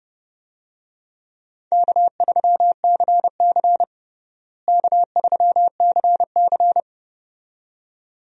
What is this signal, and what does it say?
Morse code, heard as a keyed tone, about 30 wpm, 700 Hz. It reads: K3CC K3CC
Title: USA Call Signs - 4 Characters - Repeated 2x - 30wpm